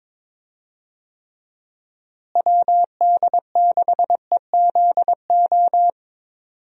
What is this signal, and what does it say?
Morse code, heard as a keyed tone, about 21 wpm, 700 Hz. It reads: WD6EZO
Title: USA Call Signs - 6 Characters - Repeated 3x - 22wpm